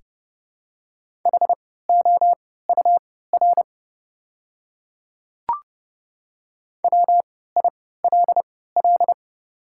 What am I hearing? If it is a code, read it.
Morse code, heard as a keyed tone, about 30 wpm, spaced as for 17 wpm, 700 Hz. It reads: HOUR WILL
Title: 3x Character Spacing - Top 500 Words - 30wpm